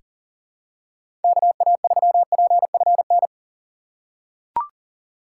Morse code, heard as a keyed tone, about 40 wpm, 700 Hz. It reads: KA3PFN
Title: USA Call Signs - 6 Characters - 2x Character Spacing - 40wpm